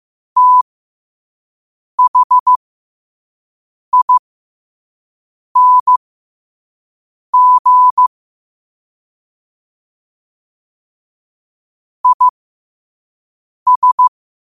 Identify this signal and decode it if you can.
Morse code, heard as a keyed tone, about 15 wpm, spaced as for 5 wpm, 1000 Hz. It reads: THING IS
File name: qotdfiles / 5wpm